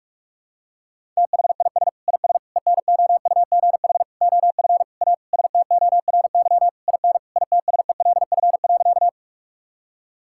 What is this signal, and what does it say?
Morse code, heard as a keyed tone, about 46 wpm, 700 Hz. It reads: THIS IS ENOUGH OF A STORY IN ITSELF.
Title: Sentences from Top 1000 Words - Review Part 5 - 45wpm